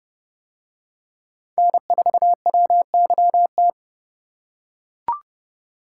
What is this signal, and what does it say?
Morse code, heard as a keyed tone, about 27 wpm, 700 Hz. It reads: N4WYT